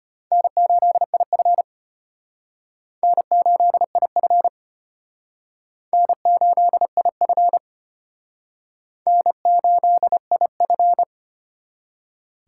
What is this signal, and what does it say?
Morse code, heard as a keyed tone, about 32 wpm, 700 Hz. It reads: N8IF N8IF N8IF N8IF